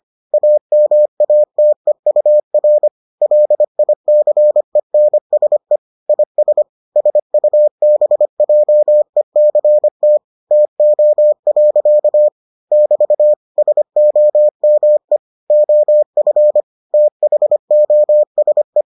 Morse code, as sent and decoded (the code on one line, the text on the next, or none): .- -- .- - . ..- .-. / .-.. .. -.-. -. ... / .. ... / ... ..- -... .--- . -.-. - / - --- .-.-.- / -...- / ... --- -- . / --- ..-. / - .... --- ... .
AMATEUR LICENSE IS SUBJECT TO. <BT> SOME OF THOSE